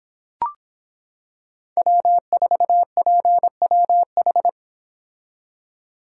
Morse code, heard as a keyed tone, about 26 wpm, 700 Hz. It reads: W4PWH